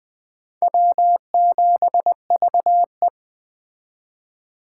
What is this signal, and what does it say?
Morse code, heard as a keyed tone, about 20 wpm, 700 Hz. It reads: W7VE
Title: Speed Racing - USA 1x2 Call Signs 17wpm